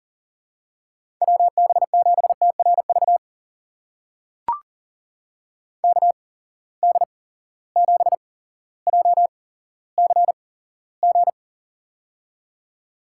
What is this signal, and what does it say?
Morse code, heard as a keyed tone, about 40 wpm, spaced as for 17 wpm, 700 Hz. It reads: WB7TRV KD7JCG
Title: USA Call Signs - 6 Characters - 8x Character Spacing - 40wpm